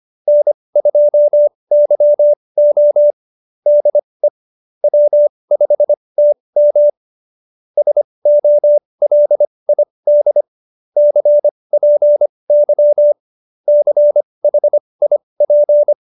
Morse code, heard as a keyed tone, about 25 wpm, spaced as for 20 wpm, 590 Hz. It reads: N2YO DE W5TM SOLID CPY CHIP
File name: QSO316 20